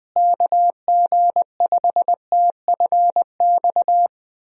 Morse code, heard as a keyed tone, about 20 wpm, 700 Hz. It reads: KG5TFX
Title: USA Call Signs - 6 Characters - Repeated 2x - 20wpm